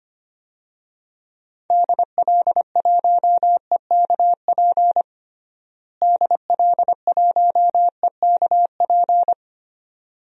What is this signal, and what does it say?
Morse code, heard as a keyed tone, about 25 wpm, 700 Hz. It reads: DL1EKP DL1EKP
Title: DXCC-100 Call Signs - 6 Characters - Repeated 2x - 25wpm